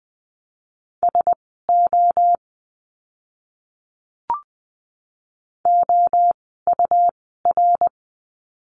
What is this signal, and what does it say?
Morse code, heard as a keyed tone, about 20 wpm, spaced as for 14 wpm, 700 Hz. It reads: SO OUR
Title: 230 Lesson 37f - Letter x - Words 20wpm